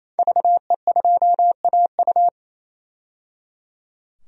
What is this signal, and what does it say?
Morse code, heard as a keyed tone, about 28 wpm, 700 Hz. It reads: VE2AU
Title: DXCC-100 Call Signs - 5 Characters - 28wpm